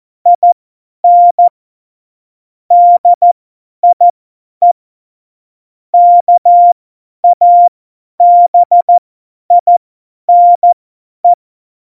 Morse code, heard as a keyed tone, about 14 wpm, spaced as for 10 wpm, 700 Hz. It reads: IN DIE KABINE
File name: Morsekurs / Witze / Witz 25. Februar 2020, 50 F70